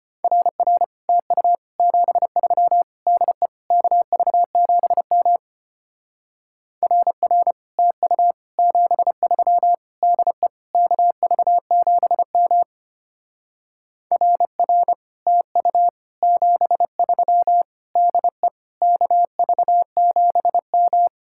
Morse code, heard as a keyed tone, about 31 wpm, 700 Hz. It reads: RR TU 73 DE KV7M RR TU 73 DE KV7M RR TU 73 DE KV7M
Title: Speed-Racing Common QSO Phrases - 25wpm